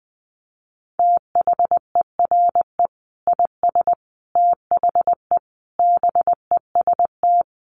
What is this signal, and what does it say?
Morse code, heard as a keyed tone, about 20 wpm, 700 Hz. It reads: THERE IS THE BEST
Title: Sentences from Top 1500 Words 20wpm